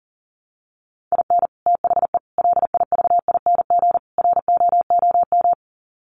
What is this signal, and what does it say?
Morse code, heard as a keyed tone, about 40 wpm, 700 Hz. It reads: IN THE LIVING ROOM
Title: Sets of 4 Words 40wpm